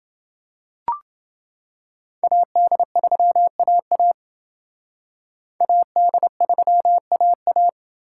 Morse code, heard as a keyed tone, about 29 wpm, 700 Hz. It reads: AD3AA AD3AA